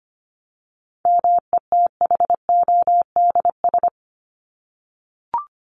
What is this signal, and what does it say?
Morse code, heard as a keyed tone, about 25 wpm, 700 Hz. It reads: METHODS
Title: ES:S Suffix - Top 100 Words with Base Words 25wpm